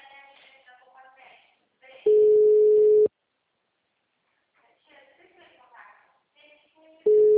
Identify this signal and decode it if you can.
Ringback tone (Europe, 425 Hz), with speech.